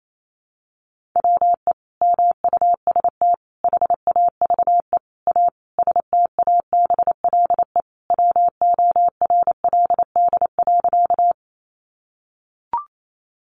Morse code, heard as a keyed tone, about 28 wpm, 700 Hz. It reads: WE MUST HAVE A STABLE WORLD.
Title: Sentences from Top 1500 Words 28wpm